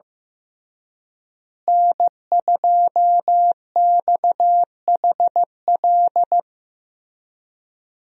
Morse code, heard as a keyed tone, about 15 wpm, 700 Hz. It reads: N2XHL
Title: USA Call Signs - 5 Characters - Repeated 2x - 15wpm